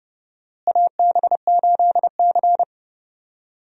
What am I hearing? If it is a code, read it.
Morse code, heard as a keyed tone, about 31 wpm, 700 Hz. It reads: AB8C
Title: USA Call Signs - 4 Characters - Repeated 2x - 30wpm